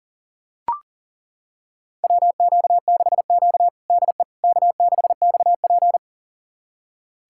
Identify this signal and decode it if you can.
Morse code, heard as a keyed tone, about 40 wpm, 700 Hz. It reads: WQ6Q DE K6XP
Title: USA Callsign DE Callsign - 4 Characters - 40wpm